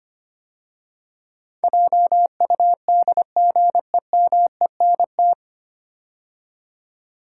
Morse code, heard as a keyed tone, about 25 wpm, 690 Hz. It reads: JUDGEMENT
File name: practice / 25wpm / 9 Letter Words - First 3 Letters Given - 25wpm